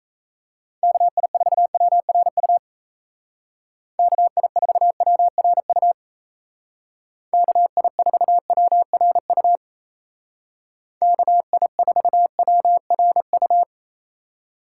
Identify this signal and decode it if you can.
Morse code, heard as a keyed tone, about 35 wpm, 700 Hz. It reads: KI4WRU KI4WRU KI4WRU KI4WRU
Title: Speed Racing - USA 2x3 Call Signs 28wpm